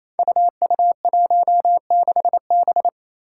Morse code, heard as a keyed tone, about 27 wpm, 700 Hz. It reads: UU16B
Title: Five Challenging Letter-Number Combinations 28wpm